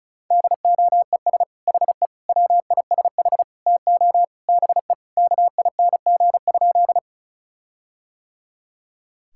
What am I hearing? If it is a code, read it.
Morse code, heard as a keyed tone, about 36 wpm, 700 Hz. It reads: DOES HE WISH TO BE KING?
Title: Sentences from Top 1600 Words 35wpm